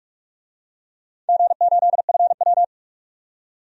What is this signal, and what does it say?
Morse code, heard as a keyed tone, about 44 wpm, 700 Hz. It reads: G8FW
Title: DXCC-100 Call Signs - 4 Characters - 45wpm